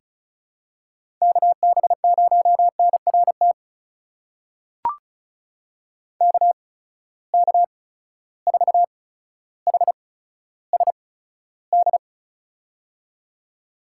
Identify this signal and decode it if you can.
Morse code, heard as a keyed tone, about 36 wpm, spaced as for 15 wpm, 700 Hz. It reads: KB0NRT KK4HSD
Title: USA Call Signs - 6 Characters - 8x Character Spacing - 35wpm